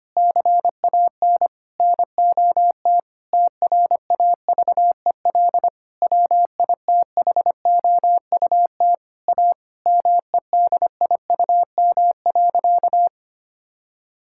Morse code, heard as a keyed tone, about 25 wpm, 700 Hz. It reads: CAN NOT TRAVEL WITHOUT A MEDIUM.
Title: Sentences from Top 1000 Words - Review Part 6 - 25wpm